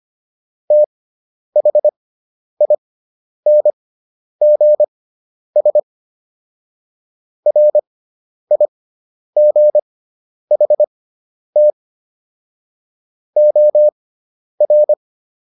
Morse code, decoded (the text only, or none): THINGS RIGHT OR